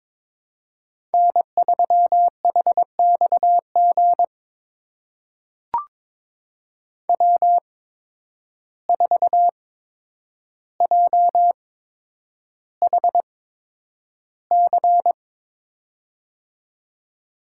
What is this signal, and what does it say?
Morse code, heard as a keyed tone, about 22 wpm, spaced as for 9 wpm, 700 Hz. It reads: N3HXG W4JHC
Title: USA Call Signs - 5 Characters - 8x Character Spacing - 22wpm